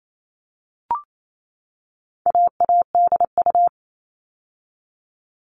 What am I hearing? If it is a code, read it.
Morse code, heard as a keyed tone, about 28 wpm, 700 Hz. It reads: AADU
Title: All Cut Numbers - 4-Digit 28wpm